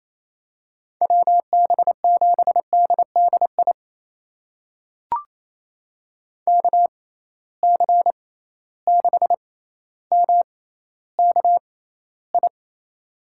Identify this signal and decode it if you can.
Morse code, heard as a keyed tone, about 28 wpm, spaced as for 14 wpm, 700 Hz. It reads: WB7DDI KC6MKI
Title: USA Call Signs - 6 Characters - 6x Character Spacing - 28wpm